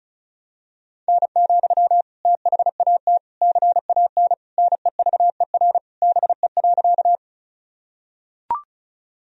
Morse code, heard as a keyed tone, about 36 wpm, 700 Hz. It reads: N, THAT CAN NEVER BE.
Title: Sentences from Top 300 Words and Abbreviations 35wpm